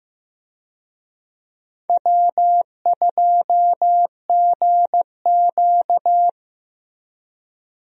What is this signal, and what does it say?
Morse code, heard as a keyed tone, about 15 wpm, 700 Hz. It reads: W2GQ
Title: USA Call Signs - 4 Characters - Repeated 3x - 15wpm